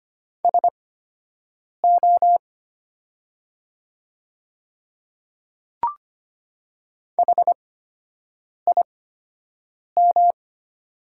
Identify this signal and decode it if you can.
Morse code, heard as a keyed tone, about 25 wpm, spaced as for 7 wpm, 700 Hz. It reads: SO HIM
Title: 8x Character Spacing - Top 200 Words - 25wpm